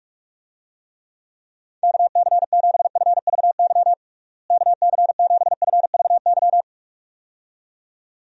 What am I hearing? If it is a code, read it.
Morse code, heard as a keyed tone, about 41 wpm, 700 Hz. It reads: KC7FVY KC7FVY
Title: USA Call Signs - 6 Characters - Repeated 2x - 45wpm